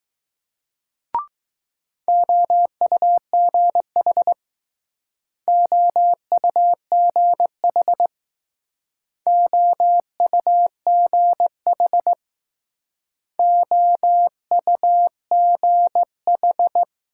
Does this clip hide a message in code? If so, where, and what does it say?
Yes: Morse code from 2.1 to 16.8 s, heard as a keyed tone, about 19 wpm, 700 Hz. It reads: OUGH OUGH OUGH OUGH